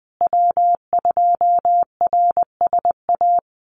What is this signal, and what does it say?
Morse code, heard as a keyed tone, about 20 wpm, 700 Hz. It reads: W2RSA
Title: USA Call Signs Repeated 3x 20wpm